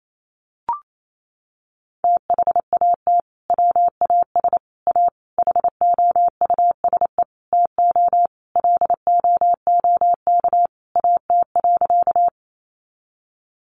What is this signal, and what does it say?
Morse code, heard as a keyed tone, about 28 wpm, 700 Hz. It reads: THAT WAS A HOUSE TO LOOK AT.
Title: Sentences from Top 200 Words 28wpm